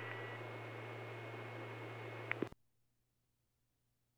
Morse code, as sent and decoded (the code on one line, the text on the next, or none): none